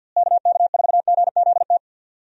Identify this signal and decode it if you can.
Morse code, heard as a keyed tone, about 52 wpm, 700 Hz. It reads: KK4GZT
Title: USA Call Signs - 6 Characters - Repeated 3x - 50wpm